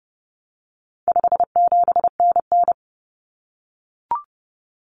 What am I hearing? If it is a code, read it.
Morse code, heard as a keyed tone, about 30 wpm, 700 Hz. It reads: 57NN
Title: Common Cut Numbers - 4-Digit 30wpm